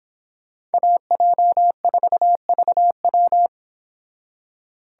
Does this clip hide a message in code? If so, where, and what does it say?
Yes: Morse code from 0.7 to 3.5 s, heard as a keyed tone, about 31 wpm, 700 Hz. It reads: AJ4VW